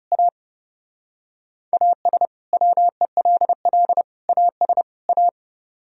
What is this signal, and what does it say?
Morse code, heard as a keyed tone, about 37 wpm, 690 Hz. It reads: A AS WELL AS A
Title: Speed-Racing - Sets of 4 Words 30wpm